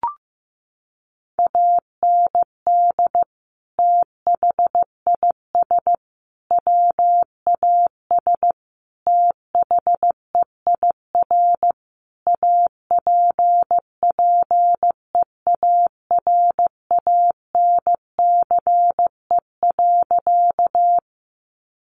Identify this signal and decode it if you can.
Morse code, heard as a keyed tone, about 15 wpm, 700 Hz. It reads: AND THIS WAS THEIR APPEARANCE.
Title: Sentences from Top 1100 Words 15wpm